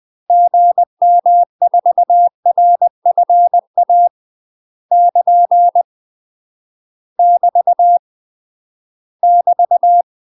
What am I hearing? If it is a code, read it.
Morse code, heard as a keyed tone, about 20 wpm, 700 Hz. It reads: GM4RFA <KN> <BT> <BT>
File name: UK MASTER QSOs TRAINING / 20WPM / UK MASTER QSOs 0059